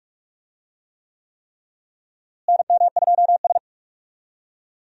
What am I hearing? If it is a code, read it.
Morse code, heard as a keyed tone, about 46 wpm, 700 Hz. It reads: NM2S
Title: USA Call Signs - 4 Characters - Repeated 3x - 45wpm